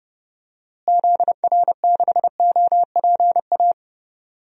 Morse code, heard as a keyed tone, about 30 wpm, 700 Hz. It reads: ZR6OPA
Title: DXCC-100 Call Signs - 6 Characters - Repeated 2x - 30wpm